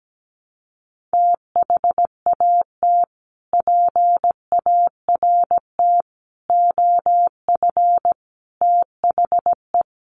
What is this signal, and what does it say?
Morse code, heard as a keyed tone, about 17 wpm, 700 Hz. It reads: THAT PART OF THE
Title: Sets of 5 Words Encore2 17wpm